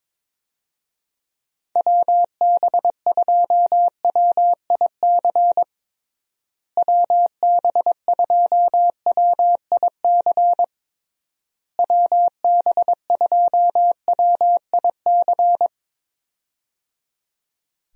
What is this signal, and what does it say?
Morse code, heard as a keyed tone, about 21 wpm, 700 Hz. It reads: WB2WIC WB2WIC WB2WIC
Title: USA Call Signs - 6 Characters - Repeated 3x - 22wpm